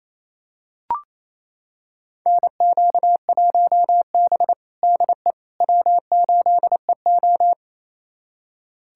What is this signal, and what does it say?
Morse code, heard as a keyed tone, about 28 wpm, 700 Hz. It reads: NQ1B DE W8EO
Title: USA Callsign DE Callsign - 4 Characters - 28wpm